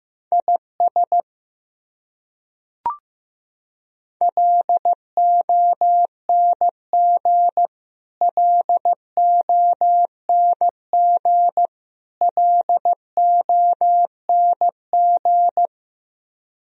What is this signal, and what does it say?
Morse code, heard as a keyed tone, about 15 wpm, 700 Hz. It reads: IS LONG LONG LONG
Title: Top 100 Words Repeated 15wpm